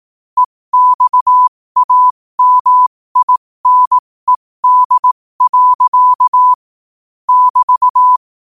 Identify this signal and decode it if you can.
Morse code, heard as a keyed tone, about 18 wpm, spaced as for 15 wpm, 1000 Hz. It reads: EXAMINED. <BT>
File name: qotdfiles / 15wpm